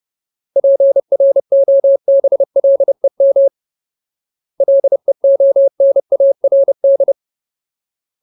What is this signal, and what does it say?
Morse code, heard as a keyed tone, about 30 wpm, 550 Hz. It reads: PROBLEM LEONARD
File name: Ultimate Word List / 30WPM Ultimate